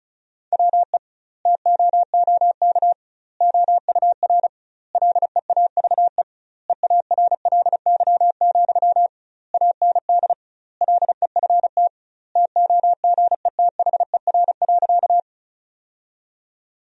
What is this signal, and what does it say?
Morse code, heard as a keyed tone, about 36 wpm, 700 Hz. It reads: WE TOOK OUR LEAVE EARLY, AND LEFT TOGETHER.
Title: 2x Word-Spacing - Sentences from Top 400 Words - 35wpm